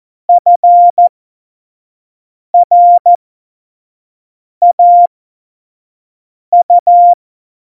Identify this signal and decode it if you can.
Morse code, heard as a keyed tone, about 14 wpm, spaced as for 5 wpm, 700 Hz. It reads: FRAU
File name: Morsekurs / Witze / Witz 21 Juli 2020, 25 F70